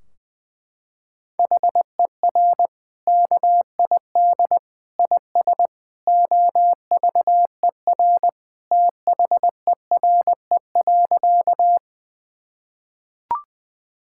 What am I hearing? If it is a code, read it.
Morse code, heard as a keyed tone, about 20 wpm, 700 Hz. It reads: HER KID IS OVER THERE.
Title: Sentences from Top 1600 Words 20wpm